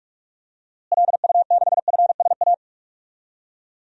Morse code, heard as a keyed tone, about 46 wpm, 700 Hz. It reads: LU6FSA
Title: DXCC-100 Call Signs - 6 Characters - Repeated 3x - 45wpm